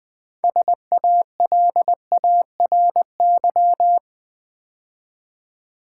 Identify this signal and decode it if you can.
Morse code, heard as a keyed tone, about 20 wpm, 700 Hz. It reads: SALARY
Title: Build A Word - 3 Letters At A Time - 6-Letter Words - 20wpm